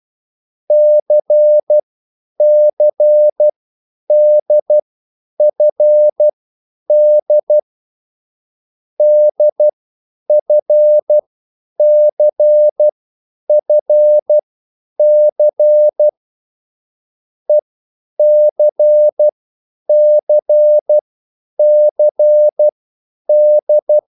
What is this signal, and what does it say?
Morse code, heard as a keyed tone, about 12 wpm, spaced as for 9 wpm, 600 Hz. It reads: CCDFD DFCFC ECCCD
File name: QSL / CW lesson